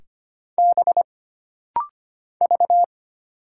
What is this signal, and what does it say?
Morse code, heard as a keyed tone, about 25 wpm, 700 Hz. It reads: B V
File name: Single Character (Letter:Number:Punctuation:Prosign) - Rapid-Fire - 25wpm